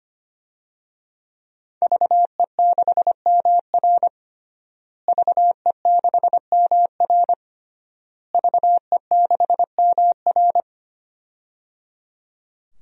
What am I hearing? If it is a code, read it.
Morse code, heard as a keyed tone, about 25 wpm, 700 Hz. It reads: VE6MR VE6MR VE6MR
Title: DXCC-100 Call Signs - 5 Characters - Repeated 3x - 25wpm